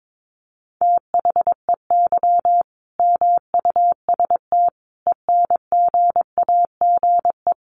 Morse code, heard as a keyed tone, about 22 wpm, 700 Hz. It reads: THEY MUST ENGAGE
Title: Sentences from Top 1500 Words 22wpm